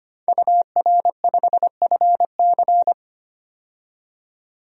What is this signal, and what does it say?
Morse code, heard as a keyed tone, about 25 wpm, 700 Hz. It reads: UR5FC
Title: DXCC-100 Call Signs - 5 Characters - 25wpm